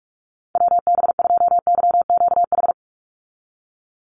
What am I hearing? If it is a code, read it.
Morse code, heard as a keyed tone, about 44 wpm, 700 Hz. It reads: WB2YQH
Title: Call Signs Repeated 2x - 45wpm